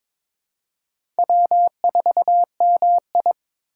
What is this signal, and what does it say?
Morse code, heard as a keyed tone, about 22 wpm, 700 Hz. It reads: W4MI
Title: USA Call Signs - 4 Characters - 3x Character Spacing - 22wpm